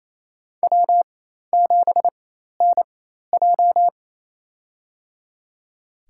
Morse code, heard as a keyed tone, about 27 wpm, spaced as for 13 wpm, 700 Hz. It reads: W7NJ
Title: USA Call Signs - 4 Characters - 4x Character Spacing - 28wpm